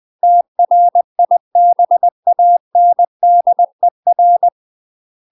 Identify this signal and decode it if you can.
Morse code, heard as a keyed tone, about 20 wpm, 700 Hz. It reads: TRIBANDER
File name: UK MASTER QSOs TRAINING / 20WPM / UK MASTER QSOs 0032